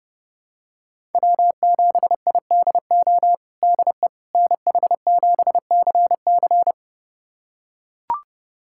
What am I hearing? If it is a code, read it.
Morse code, heard as a keyed tone, about 32 wpm, 700 Hz. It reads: W7IDO DE NH7CC
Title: USA Callsign DE Callsign - 5 Characters - 30wpm